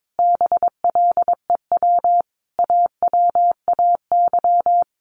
Morse code, heard as a keyed tone, about 22 wpm, 700 Hz. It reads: BLEW AWAY
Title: A Sound Of Thunder 22wpm